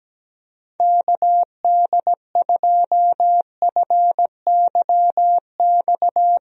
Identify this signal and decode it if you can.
Morse code, heard as a keyed tone, about 17 wpm, 700 Hz. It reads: KD2FYX
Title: USA Call Signs - 6 Characters - Repeated 2x - 17wpm